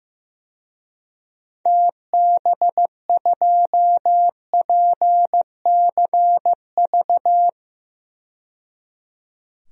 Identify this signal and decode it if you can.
Morse code, heard as a keyed tone, about 15 wpm, 700 Hz. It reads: TB2PCV
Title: DXCC-100 Call Signs - 6 Characters - 15wpm